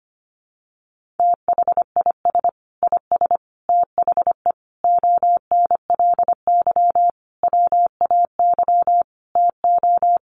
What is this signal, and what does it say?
Morse code, heard as a keyed tone, about 25 wpm, 700 Hz. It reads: THIS IS THE ONLY WAY TO
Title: Sentences from Top 1400 Words 25wpm